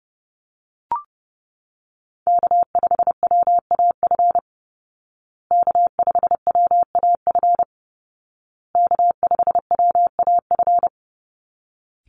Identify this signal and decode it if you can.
Morse code, heard as a keyed tone, about 30 wpm, 700 Hz. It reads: K5WAF K5WAF K5WAF
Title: Call Signs Repeated 2x - 30wpm